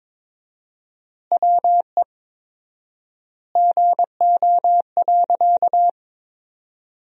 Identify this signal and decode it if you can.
Morse code, heard as a keyed tone, about 22 wpm, 700 Hz. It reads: WE GO.
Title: 4x Word-Spacing - Sentences from Top 200 Words - 22wpm